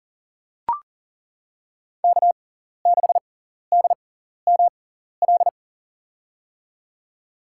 Morse code, heard as a keyed tone, about 40 wpm, spaced as for 14 wpm, 700 Hz. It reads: K6DML
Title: USA Call Signs - 6x Character Spacing - 40wpm